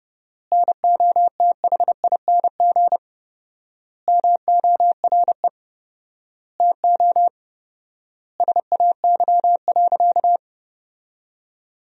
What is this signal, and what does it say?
Morse code, heard as a keyed tone, about 30 wpm, 700 Hz. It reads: NOTHING MORE TO SAY.
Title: 4x Word-Spacing - Sentences from Top 500 Words - 30wpm